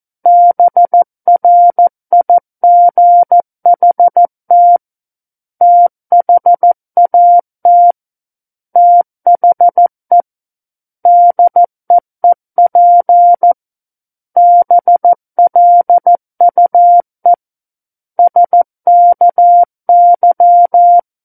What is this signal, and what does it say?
Morse code, heard as a keyed tone, about 14 wpm, 700 Hz. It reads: BRIGHT THAT THE DEEP BLUE SKY
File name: Book One / 05-Chapter 5 - 14 WPM